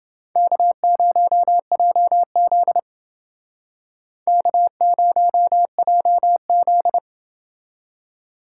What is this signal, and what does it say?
Morse code, heard as a keyed tone, about 29 wpm, 700 Hz. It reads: K0JZ K0JZ